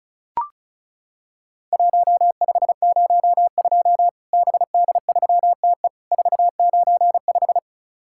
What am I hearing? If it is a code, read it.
Morse code, heard as a keyed tone, about 36 wpm, 700 Hz. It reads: 1502 BD3TE 495
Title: Worked All Europe DX Contest - QTC Exchange - 35wpm